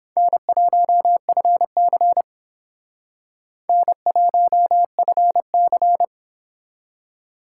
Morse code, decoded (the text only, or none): N1FC N1FC